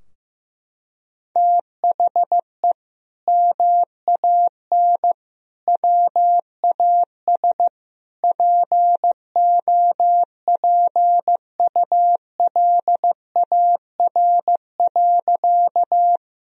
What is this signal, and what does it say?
Morse code, heard as a keyed tone, about 15 wpm, 700 Hz. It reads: THE MAN WAS POPULAR.
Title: Sentences from Top 1000 Words - Review Part 3 - 15wpm